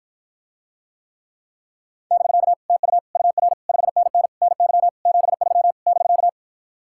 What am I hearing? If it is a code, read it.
Morse code, heard as a keyed tone, about 54 wpm, 700 Hz. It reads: <BK> TU UR 5NN NY 73 <BK>